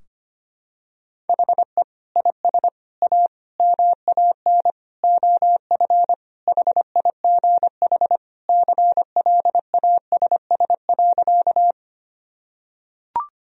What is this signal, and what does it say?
Morse code, heard as a keyed tone, about 25 wpm, 700 Hz. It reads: HE IS A MAN OF HIGH CLASS.
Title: Sentences from Top 1000 Words - Review Part 2 - 25wpm